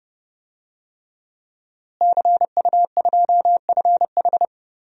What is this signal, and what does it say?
Morse code, heard as a keyed tone, about 32 wpm, 700 Hz. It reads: CU2FH
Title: DXCC-100 Call Signs - 5 Characters - Repeated 3x - 30wpm